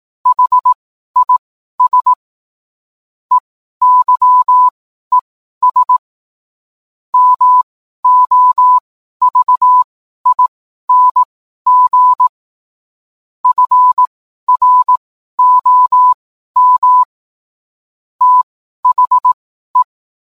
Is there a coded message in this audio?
Yes: Morse code from 0.3 to 19.8 s, heard as a keyed tone, about 18 wpm, spaced as for 12 wpm, 1000 Hz. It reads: HIS EYES MOVING FROM THE